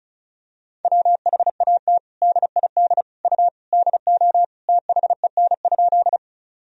Morse code, heard as a keyed tone, about 36 wpm, 700 Hz. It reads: WHAT DID U DO THEN?
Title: Sentences from Top 100 Words and Abbreviations 35wpm